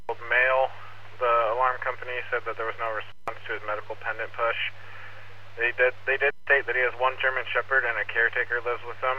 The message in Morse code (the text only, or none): none